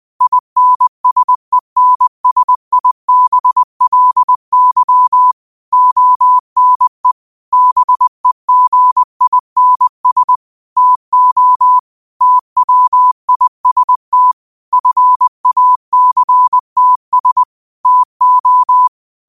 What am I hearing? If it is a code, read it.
Morse code, heard as a keyed tone, about 20 wpm, 1000 Hz. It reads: INSENSIBLY ONE BEGINS TO TWIST FACTS TO